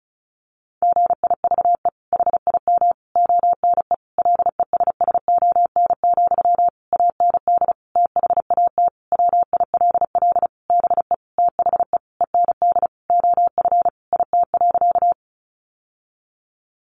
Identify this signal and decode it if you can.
Morse code, heard as a keyed tone, about 34 wpm, 700 Hz. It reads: GIVE HIM ONE LESSON, AND THAT WILL BE THE END OF IT.
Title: Sentences from Top 1200 Words 35wpm